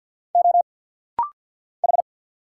Morse code, heard as a keyed tone, about 50 wpm, 690 Hz. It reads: O H
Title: Single Letters - Rapid-Fire 50wpm